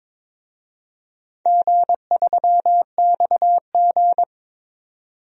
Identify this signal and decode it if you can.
Morse code, heard as a keyed tone, about 21 wpm, 700 Hz. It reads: G3XG
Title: DXCC-100 Call Signs - 4 Characters - 22wpm